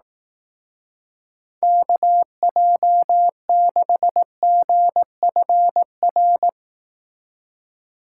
Morse code, heard as a keyed tone, about 18 wpm, 700 Hz. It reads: KJ6GFR
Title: Speed Racing - USA 2x3 Call Signs 15wpm